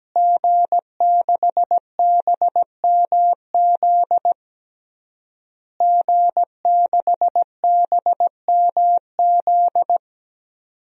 Morse code, heard as a keyed tone, about 17 wpm, 700 Hz. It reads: G6BMZ G6BMZ
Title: DXCC-100 Call Signs - 5 Characters - Repeated 3x - 17wpm